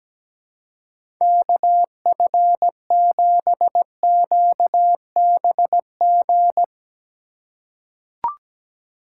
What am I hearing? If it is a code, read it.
Morse code, heard as a keyed tone, about 17 wpm, 700 Hz. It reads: KF7QBG
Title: USA Call Signs - 6 Characters - 4x Character Spacing - 17wpm